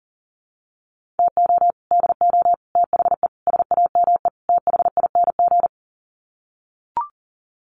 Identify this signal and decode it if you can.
Morse code, heard as a keyed tone, about 40 wpm, 700 Hz. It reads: TO DO THE SAME THING
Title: Sets of 5 Words Encore 40wpm